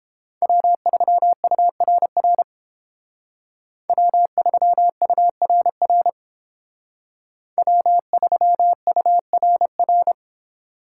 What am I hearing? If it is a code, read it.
Morse code, heard as a keyed tone, about 29 wpm, 700 Hz. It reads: W3URR W3URR W3URR